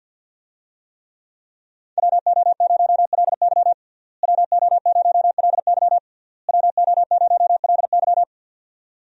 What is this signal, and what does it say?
Morse code, heard as a keyed tone, about 52 wpm, 700 Hz. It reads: WO0LY WO0LY WO0LY
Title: USA Call Signs - 5 Characters - Repeated 3x - 50wpm